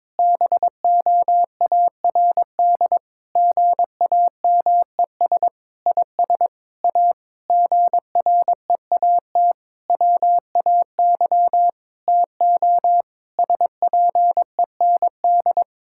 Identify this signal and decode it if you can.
Morse code, heard as a keyed tone, about 22 wpm, 700 Hz. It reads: BOARD GAMES IS A GREAT WAY TO SPEND
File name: Difficult Sentences with Hints - 22wpm